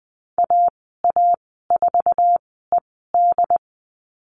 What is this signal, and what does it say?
Morse code, heard as a keyed tone, about 20 wpm, spaced as for 15 wpm, 700 Hz. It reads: AA4ED
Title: 076 Lesson 13g - Letter L - Call Signs 20wpm